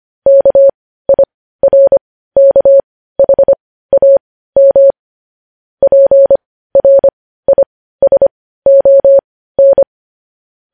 Morse code, heard as a keyed tone, about 25 wpm, spaced as for 15 wpm, 550 Hz. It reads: KIRKHAM PRISON
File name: podcast / News headlines-1x-25x15